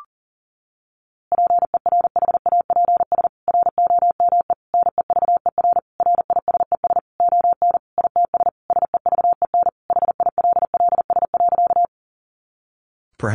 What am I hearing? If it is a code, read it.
Morse code, heard as a keyed tone, about 37 wpm, 700 Hz. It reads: PERHAPS ROME NEVER RISES ON ITS SEVEN HILLS.
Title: A Sound Of Thunder 40wpm